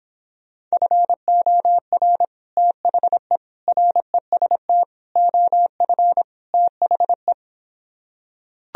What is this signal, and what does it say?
Morse code, heard as a keyed tone, about 26 wpm, 700 Hz. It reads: FOR THE REST OF THE